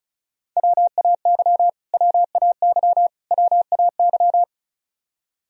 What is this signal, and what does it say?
Morse code, heard as a keyed tone, about 34 wpm, 700 Hz. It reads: WAY WAY WAY
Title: Top 100 Words Repeated 35wpm